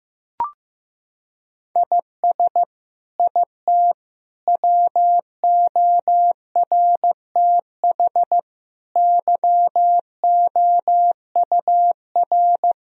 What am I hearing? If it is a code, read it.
Morse code, heard as a keyed tone, about 15 wpm, 700 Hz. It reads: IS IT WORTH YOUR
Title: Sentences from Top 1000 Words - Review Part 1 - 15wpm